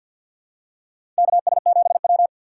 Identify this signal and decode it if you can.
Morse code, heard as a keyed tone, about 53 wpm, 700 Hz. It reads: KS7W
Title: USA Call Signs - 4 Characters - 3x Character Spacing - 50wpm